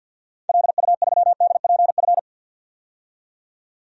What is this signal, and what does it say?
Morse code, heard as a keyed tone, about 52 wpm, 700 Hz. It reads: LU3DPF
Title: DXCC-100 Call Signs - 6 Characters - 50wpm